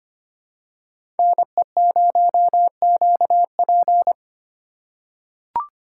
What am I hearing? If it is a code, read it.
Morse code, heard as a keyed tone, about 25 wpm, 700 Hz. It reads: NE0QP
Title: USA Call Signs - 5 Characters - 4x Character Spacing - 25wpm